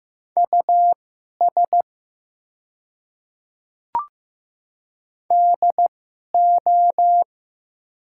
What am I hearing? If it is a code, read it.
Morse code, heard as a keyed tone, about 15 wpm, spaced as for 11 wpm, 700 Hz. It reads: US DO
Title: 2x Character Spacing - Top 300 Words - 15wpm